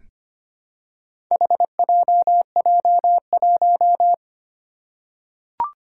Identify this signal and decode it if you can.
Morse code, heard as a keyed tone, about 25 wpm, 700 Hz. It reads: HJJ1
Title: Four Challenging Letter-Number Combinations 25wpm